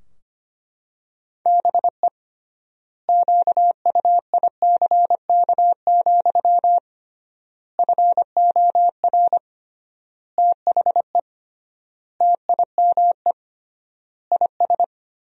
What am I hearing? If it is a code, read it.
Morse code, heard as a keyed tone, about 25 wpm, 700 Hz. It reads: BE QUICK, FOR THE TIME IS
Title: 3x Word-Spacing - Sentences from Top 500 Words - 25wpm